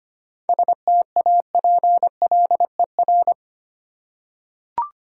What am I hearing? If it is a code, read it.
Morse code, heard as a keyed tone, about 25 wpm, 700 Hz. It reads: STAPLER
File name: practice / 25wpm / Difficult Words with Hints - 25wpm